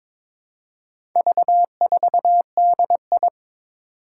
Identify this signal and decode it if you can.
Morse code, heard as a keyed tone, about 21 wpm, 700 Hz. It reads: V4DI
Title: Four Challenging Letter-Number Combinations 22wpm